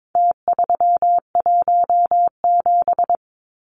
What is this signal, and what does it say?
Morse code, heard as a keyed tone, about 22 wpm, 700 Hz. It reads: T317
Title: Numbers - Contest Style - 4-Digit 22wpm